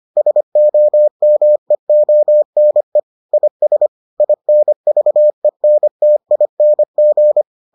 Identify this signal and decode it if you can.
Morse code, heard as a keyed tone, about 25 wpm, 600 Hz. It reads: SOMEONE IS INVENTING